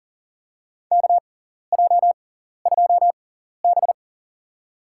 Morse code, heard as a keyed tone, about 40 wpm, spaced as for 14 wpm, 700 Hz. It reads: KJ2B
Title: USA Call Signs - 4 Characters - 6x Character Spacing - 40wpm